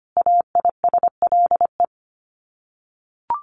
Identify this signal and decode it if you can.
Morse code, heard as a keyed tone, about 25 wpm, 700 Hz. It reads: AISLE
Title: Sets of 5 Words Encore2 25wpm